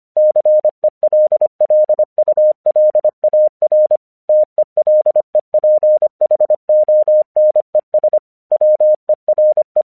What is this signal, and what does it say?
Morse code, heard as a keyed tone, about 25 wpm, 610 Hz. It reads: CELLULAR TELEPHONES WERE
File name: 1 English 25wpm / AQC06